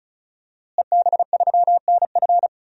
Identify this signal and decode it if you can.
Morse code, heard as a keyed tone, about 36 wpm, 700 Hz. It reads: EB3NF
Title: DXCC-100 Call Signs - 5 Characters - Repeated 3x - 35wpm